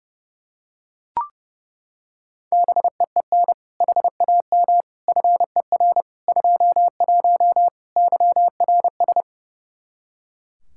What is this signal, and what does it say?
Morse code, heard as a keyed tone, about 30 wpm, 700 Hz. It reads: BEEN HAM FER 21 YRS
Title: Practice QSO - Exchange 3 - More Info - 30wpm